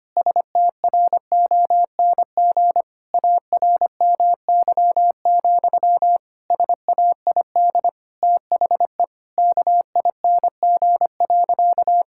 Morse code, heard as a keyed tone, about 25 wpm, 700 Hz. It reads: STRONG ARMY, SAID THE KING.
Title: Sentences from Top 1000 Words - Review Part 5 - 25wpm